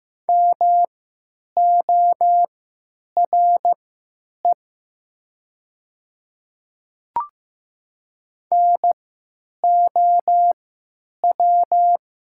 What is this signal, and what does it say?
Morse code, heard as a keyed tone, about 15 wpm, spaced as for 9 wpm, 700 Hz. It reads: MORE NOW
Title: 3x Character Spacing - Top 500 Words - 15wpm